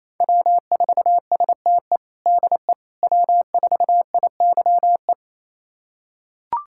Morse code, heard as a keyed tone, about 28 wpm, 700 Hz. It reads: W4STE DE W4IYE
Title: USA Callsign DE Callsign - 5 Characters - 28wpm